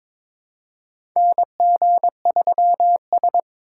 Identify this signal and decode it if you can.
Morse code, heard as a keyed tone, about 22 wpm, 700 Hz. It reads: NG3S
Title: USA Call Signs - 2x Character Spacing - 22wpm